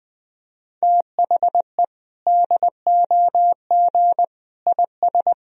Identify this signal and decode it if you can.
Morse code, heard as a keyed tone, about 21 wpm, 700 Hz. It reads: THE DOG IS
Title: Sentences from Top 400 Words and Abbreviations 20wpm